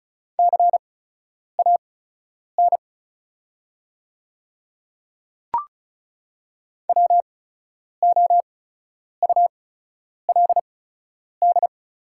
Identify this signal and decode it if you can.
Morse code, heard as a keyed tone, about 36 wpm, spaced as for 10 wpm, 700 Hz. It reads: CAN WOULD
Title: 8x Character Spacing - Top 100 Words - 35wpm